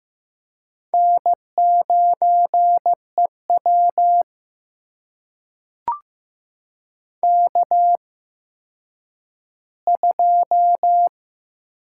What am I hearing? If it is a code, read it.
Morse code, heard as a keyed tone, about 15 wpm, spaced as for 9 wpm, 700 Hz. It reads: N9EW K2
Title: USA Call Signs - 4 Characters - 8x Character Spacing - 15wpm